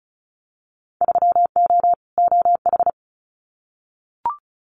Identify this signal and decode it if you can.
Morse code, heard as a keyed tone, about 36 wpm, 700 Hz. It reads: 3O OH
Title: Winter Field Day Exchanges 35wpm